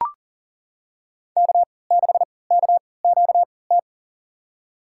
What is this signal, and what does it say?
Morse code, heard as a keyed tone, about 40 wpm, spaced as for 23 wpm, 700 Hz. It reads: K6KQT
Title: USA Call Signs - 5 Characters - 3x Character Spacing - 40wpm